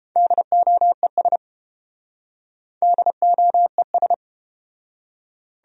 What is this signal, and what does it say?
Morse code, heard as a keyed tone, about 31 wpm, 700 Hz. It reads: DOES DOES